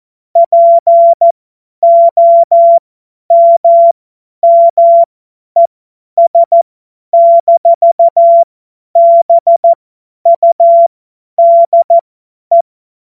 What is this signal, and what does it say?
Morse code, heard as a keyed tone, about 14 wpm, spaced as for 10 wpm, 680 Hz. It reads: POMMES-BUDE
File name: Morsekurs / Witze / Witz 10.Oktober 2023 50 F70